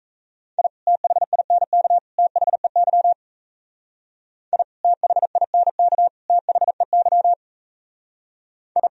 Morse code, heard as a keyed tone, about 35 wpm, 700 Hz. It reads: I THINK THEY I THINK THEY I
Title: Speed-Racing - Sets of 3 Words 28wpm